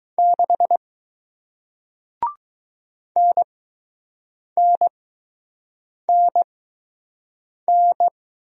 Morse code, heard as a keyed tone, about 20 wpm, spaced as for 6 wpm, 700 Hz. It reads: 6 NNNN